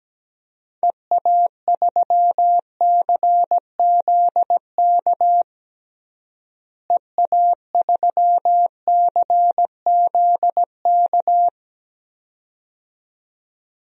Morse code, heard as a keyed tone, about 17 wpm, 700 Hz. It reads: EA3CZK EA3CZK
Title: DXCC-100 Call Signs - 6 Characters - Repeated 2x - 17wpm